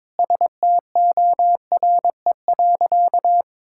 Morse code, heard as a keyed tone, about 22 wpm, 700 Hz. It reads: STORE.
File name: practice / 22wpm / Sentences with 3-Word Phrases - 22wpm